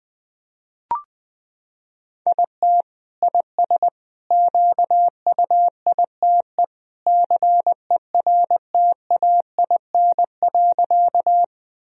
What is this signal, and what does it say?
Morse code, heard as a keyed tone, about 20 wpm, 700 Hz. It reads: IT IS QUITE CERTAIN.
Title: Sentences from Top 1000 Words - Review Part 2 - 20wpm